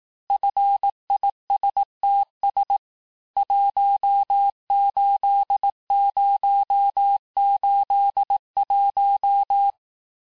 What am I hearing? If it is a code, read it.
Morse code, heard as a keyed tone, about 18 wpm, 800 Hz. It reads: FISTS 18081
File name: cours-du-lundi files / cours-1594--FIN-de-QSO-vitesse-17mpm